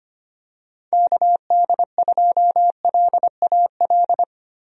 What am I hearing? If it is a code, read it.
Morse code, heard as a keyed tone, about 25 wpm, 700 Hz. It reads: KD2LAL
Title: USA Call Signs - 6 Characters - Repeated 2x - 25wpm